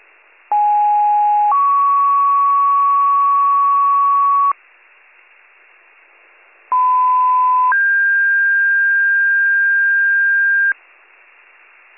Two-tone page, heard as a one-second tone, then a three-second tone, about 830 then 1120 Hz, about 1010 then 1600 Hz. It is heard from 0.5 to 10.7 s.